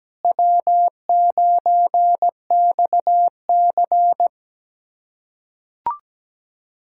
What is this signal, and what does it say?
Morse code, heard as a keyed tone, about 17 wpm, 700 Hz. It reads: W9XC
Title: K1USN SST QSO Practice - 17wpm